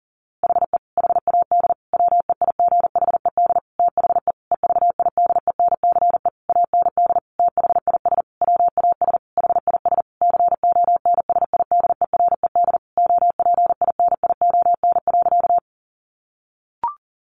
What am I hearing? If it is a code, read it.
Morse code, heard as a keyed tone, about 40 wpm, 700 Hz. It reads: HE HAD WEIGHED THE EVIDENCE AND THIS WAS HIS CONSIDERED OPINION.
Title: A Sound Of Thunder 40wpm